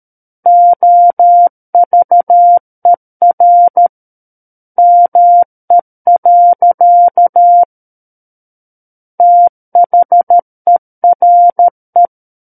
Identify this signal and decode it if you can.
Morse code, heard as a keyed tone, about 13 wpm, 700 Hz. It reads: OVER ME. THERE
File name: Book One / 04-Chapter 4 - 13 WPM